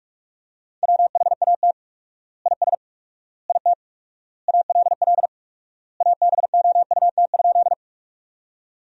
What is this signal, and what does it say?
Morse code, heard as a keyed tone, about 42 wpm, 700 Hz. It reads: WHAT IS IT ALL ABOUT?
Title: 4x Word-Spacing - Sentences from Top 100 Words - 45wpm